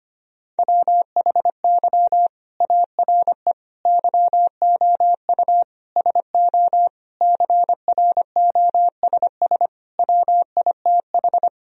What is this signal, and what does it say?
Morse code, heard as a keyed tone, about 25 wpm, 700 Hz. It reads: WHY ARE YOU SO CROSS WITH
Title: Sentences from Top 1000 Words - Review Part 1 - 25wpm